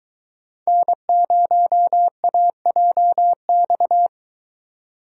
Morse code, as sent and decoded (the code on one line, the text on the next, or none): -. ----- .- .--- -..-
N0AJX